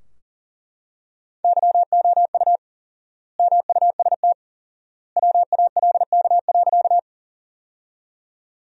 Morse code, heard as a keyed tone, about 40 wpm, 700 Hz. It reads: YOU MUST WALK.
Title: 4x Word-Spacing - Sentences from Top 300 Words - 40wpm